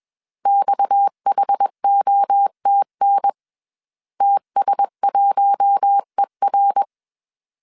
Morse code, heard as a keyed tone, about 21 wpm, 800 Hz. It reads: XHOTN TS1ER
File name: CW / DET / 1 6 1500C S C 20wpm 2116